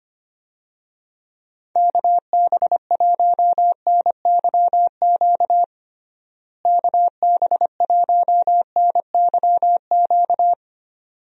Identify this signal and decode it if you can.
Morse code, heard as a keyed tone, about 25 wpm, 700 Hz. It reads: KB1NYQ KB1NYQ
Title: USA Call Signs - 6 Characters - Repeated 2x - 25wpm